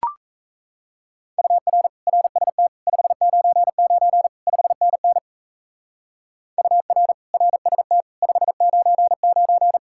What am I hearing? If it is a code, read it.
Morse code, heard as a keyed tone, about 32 wpm, 700 Hz. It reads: UR RST 599 5NN UR RST 599
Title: Speed-Racing Common QSO Phrases - 28wpm